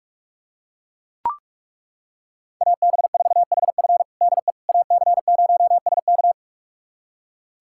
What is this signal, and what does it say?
Morse code, heard as a keyed tone, about 41 wpm, 700 Hz. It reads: AB4HF DE AC0SK
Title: USA Callsign DE Callsign - 5 Characters - 45wpm